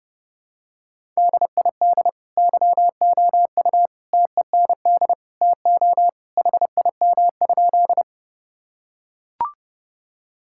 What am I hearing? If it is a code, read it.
Morse code, heard as a keyed tone, about 30 wpm, 700 Hz. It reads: DID YOU TEND TO HIM?
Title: Sentences from Top 1000 Words - Review Part 5 - 30wpm